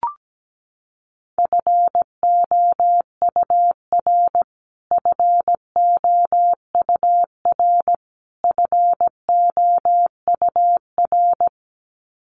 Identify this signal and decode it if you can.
Morse code, heard as a keyed tone, about 17 wpm, 700 Hz. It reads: FOUR FOUR FOUR
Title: Top 300 Words Repeated 17wpm